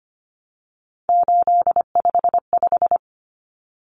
Morse code, heard as a keyed tone, about 25 wpm, 700 Hz. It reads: 855
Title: Numbers - 3-Digit 25wpm